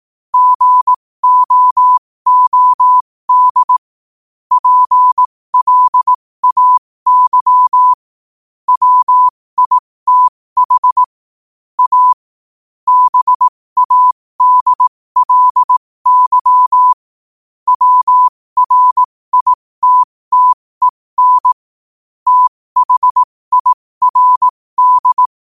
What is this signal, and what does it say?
Morse code, heard as a keyed tone, about 18 wpm, spaced as for 15 wpm, 1000 Hz. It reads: GOOD PLAY WITH A BADLY WRITTEN THIRD
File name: qotdfiles / 15wpm